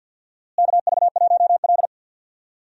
Morse code, heard as a keyed tone, about 52 wpm, 700 Hz. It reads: KV1L
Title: USA Call Signs - 4 Characters - Repeated 3x - 50wpm